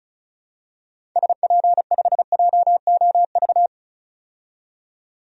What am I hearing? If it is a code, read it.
Morse code, heard as a keyed tone, about 34 wpm, 700 Hz. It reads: SP5JOV
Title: DXCC-100 Call Signs - 6 Characters - 35wpm